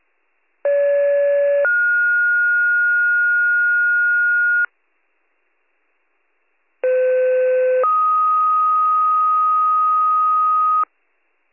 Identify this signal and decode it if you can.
Two-tone page, heard as a one-second tone, then a three-second tone, about 570 then 1430 Hz, about 510 then 1250 Hz. It is heard from 0.6 to 10.8 s.